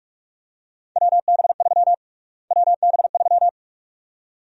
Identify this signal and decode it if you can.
Morse code, heard as a keyed tone, about 43 wpm, 700 Hz. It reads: WB3 WB3
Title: USA Call Sign Prefixes - Repeated 2x - 45wpm